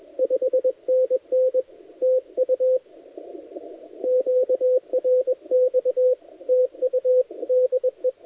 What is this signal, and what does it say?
Morse code, heard as a keyed tone, about 20 wpm, 510 Hz, with radio noise. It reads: 5NN TU QRX TU DE
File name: dlazaawansowanych / nagraniazpasma / titawaNETcontrol